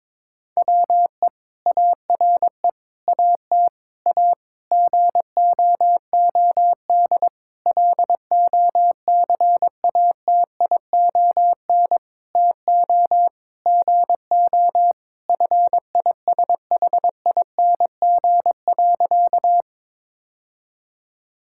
Morse code, heard as a keyed tone, about 22 wpm, 700 Hz. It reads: WE ARE AT A GOOD LOCATION TO GO FISHING.
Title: Sentences from Top 1000 Words - Review Part 6 - 22wpm